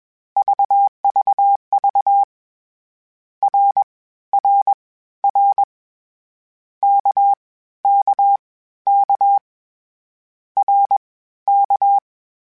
Morse code, heard as a keyed tone, about 21 wpm, spaced as for 14 wpm, 800 Hz. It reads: VVV RRR KKK RK
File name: Short Runs / Short Runs-mp3s / Balanced @20